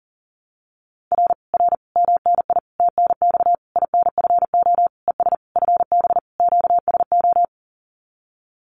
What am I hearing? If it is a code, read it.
Morse code, heard as a keyed tone, about 43 wpm, 700 Hz. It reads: R R MNI TNX INFO ES FB QSO
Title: Common QSO Phrases Encore 40wpm